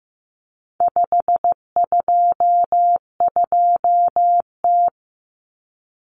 Morse code, heard as a keyed tone, about 15 wpm, 700 Hz. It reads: 522T
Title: Common Cut Numbers - 4-Digit 15wpm